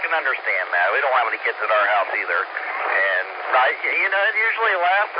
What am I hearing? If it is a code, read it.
Emergency Alert System (SAME) tones, heard as harsh buzzing bursts, with speech. It is heard from 0.4 to 4.2 s.